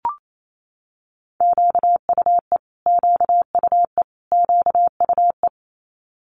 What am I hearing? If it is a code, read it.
Morse code, heard as a keyed tone, about 28 wpm, 700 Hz. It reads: QUE QUE QUE
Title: Multi-letter Phonemes Repeated 3x 28wpm